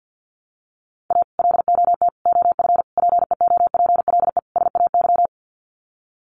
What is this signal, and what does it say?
Morse code, heard as a keyed tone, about 50 wpm, 700 Hz. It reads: A LOT OF PEOPLE SAY
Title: Sets of 5 Words Encore2 50wpm